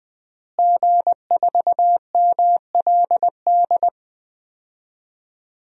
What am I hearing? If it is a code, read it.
Morse code, heard as a keyed tone, about 20 wpm, 700 Hz. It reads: G4MLD
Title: DXCC-100 Call Signs - 5 Characters - Repeated 3x - 20wpm